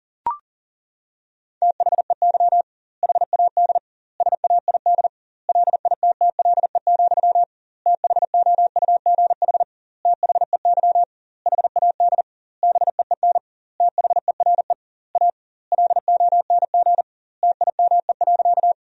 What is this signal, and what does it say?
Morse code, heard as a keyed tone, about 40 wpm, 700 Hz. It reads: THEY HAD SAID LITTLE, THOUGH THEY HAD BEEN THERE A LONG TIME.
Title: 2x Word-Spacing - Sentences from Top 400 Words - 40wpm